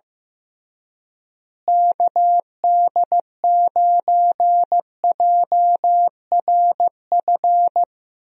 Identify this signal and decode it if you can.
Morse code, heard as a keyed tone, about 15 wpm, 700 Hz. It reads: KD9JRF
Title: USA Call Signs - 6 Characters - Repeated 3x - 15wpm